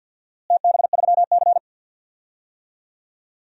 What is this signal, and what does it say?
Morse code, heard as a keyed tone, about 51 wpm, 700 Hz. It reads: TB3C
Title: DXCC-100 Call Signs - 4 Characters - Repeated 2x - 50wpm